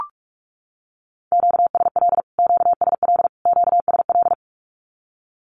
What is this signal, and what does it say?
Morse code, heard as a keyed tone, about 44 wpm, 700 Hz. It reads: QSL QSL QSL
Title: Q-Codes Repeated 3x 45wpm